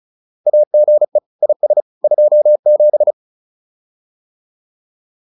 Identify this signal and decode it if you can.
Morse code, heard as a keyed tone, about 34 wpm, 600 Hz. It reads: AGE IS 27